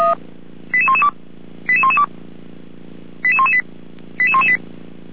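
Five-tone selective call (ZVEI), heard as quick runs of five tones, tones of about 70 ms. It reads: 88112 88112 88118 88118
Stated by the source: ZVEI five-tone call received by SDR.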